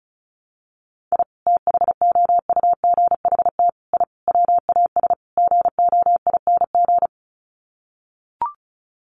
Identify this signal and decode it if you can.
Morse code, heard as a keyed tone, about 34 wpm, 700 Hz. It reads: I THOUGHT I WAS GOING
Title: Sets of 5 Words Encore2 35wpm